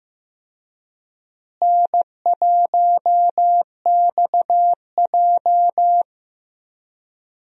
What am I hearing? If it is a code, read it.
Morse code, heard as a keyed tone, about 15 wpm, 700 Hz. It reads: N1XJ
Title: USA Call Signs - 4 Characters - Repeated 2x - 15wpm